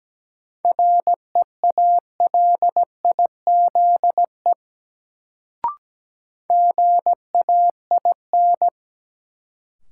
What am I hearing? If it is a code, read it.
Morse code, heard as a keyed tone, about 17 wpm, 700 Hz. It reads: REALIZE GAIN
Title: Sentences from Top 1000 Words - New Word Drill - 17wpm